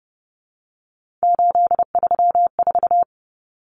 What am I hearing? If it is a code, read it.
Morse code, heard as a keyed tone, about 30 wpm, 700 Hz. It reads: 834
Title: Common Cut Numbers - 3-Digit 30wpm